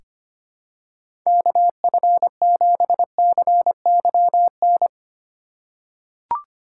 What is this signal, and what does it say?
Morse code, heard as a keyed tone, about 25 wpm, 700 Hz. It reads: KF7CYN
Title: USA Call Signs - 6 Characters - 3x Character Spacing - 25wpm